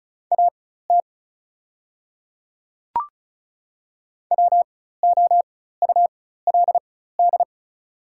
Morse code, heard as a keyed tone, about 36 wpm, spaced as for 16 wpm, 700 Hz. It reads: AT WOULD
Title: 4x Character Spacing - Top 100 Words - 35wpm